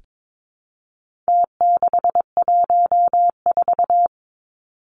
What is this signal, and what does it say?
Morse code, heard as a keyed tone, about 22 wpm, 700 Hz. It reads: T614
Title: Numbers - Contest Style - 4-Digit 22wpm